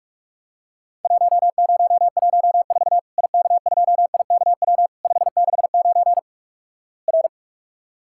Morse code, heard as a keyed tone, about 46 wpm, 700 Hz. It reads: 1014 IK2IKW 569 R
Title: Worked All Europe DX Contest - QTC Exchange - 45wpm